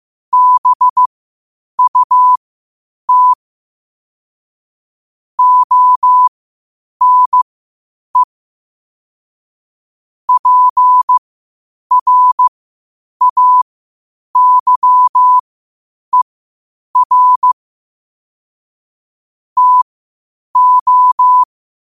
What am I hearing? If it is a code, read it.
Morse code, heard as a keyed tone, about 15 wpm, spaced as for 8 wpm, 1000 Hz. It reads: BUT ONE PRAYER TO
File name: qotdfiles / 8wpm